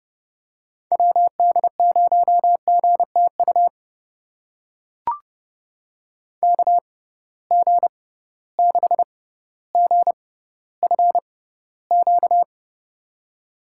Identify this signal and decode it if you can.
Morse code, heard as a keyed tone, about 30 wpm, spaced as for 15 wpm, 700 Hz. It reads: WD0GTU KG6GFQ
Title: USA Call Signs - 6 Characters - 6x Character Spacing - 30wpm